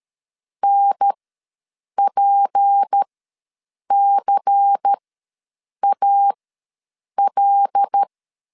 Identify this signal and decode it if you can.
Morse code, heard as a keyed tone, about 13 wpm, spaced as for 7 wpm, 800 Hz. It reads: NPCAL
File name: LEZIONI / 6 PKX0 / 6 P ws 12wpm 3117